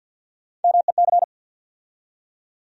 Morse code, heard as a keyed tone, about 52 wpm, 700 Hz. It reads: MEC
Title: USA Call Sign Postfixes - 3 Letters 50wpm